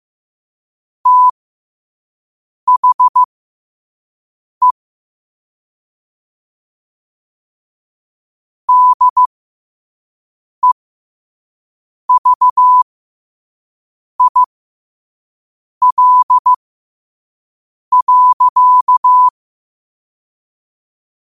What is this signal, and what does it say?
Morse code, heard as a keyed tone, about 15 wpm, spaced as for 5 wpm, 1000 Hz. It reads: THE DEVIL.